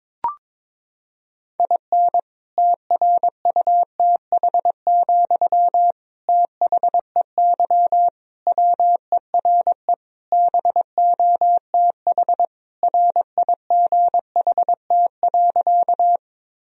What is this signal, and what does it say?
Morse code, heard as a keyed tone, about 21 wpm, 700 Hz. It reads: IN TRUTH, THEY WERE BOTH RIGHT.
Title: Sentences from Top 1000 Words - Review Part 1 - 22wpm